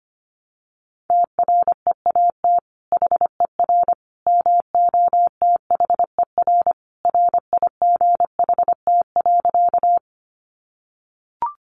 Morse code, heard as a keyed tone, about 25 wpm, 700 Hz. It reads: TREAT HER MOTHER RIGHT.
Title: Sentences from Top 1300 Words 25wpm